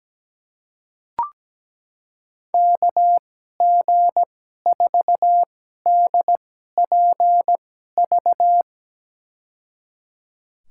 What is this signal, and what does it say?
Morse code, heard as a keyed tone, about 17 wpm, spaced as for 12 wpm, 700 Hz. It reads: KG4DPV
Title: USA Call Signs - 6 Characters - 2x Character Spacing - 17wpm